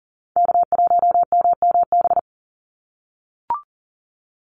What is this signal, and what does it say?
Morse code, heard as a keyed tone, about 43 wpm, 700 Hz. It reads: K1MMB
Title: Call Signs - 40wpm